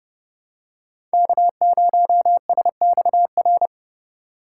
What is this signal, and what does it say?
Morse code, heard as a keyed tone, about 30 wpm, 700 Hz. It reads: K0SXR
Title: USA Call Signs - 5 Characters - 3x Character Spacing - 30wpm